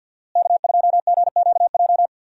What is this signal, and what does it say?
Morse code, heard as a keyed tone, about 50 wpm, 700 Hz. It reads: K2GQJ
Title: USA Call Signs - 5 Characters - Repeated 2x - 50wpm